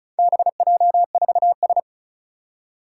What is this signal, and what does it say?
Morse code, heard as a keyed tone, about 36 wpm, 700 Hz. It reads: BJ4S
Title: Four Challenging Letter-Number Combinations 35wpm